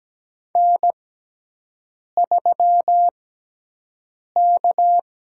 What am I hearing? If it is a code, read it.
Morse code, heard as a keyed tone, about 17 wpm, spaced as for 6 wpm, 700 Hz. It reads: N3K
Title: USA Call Signs - 5 Characters - 6x Character Spacing - 17wpm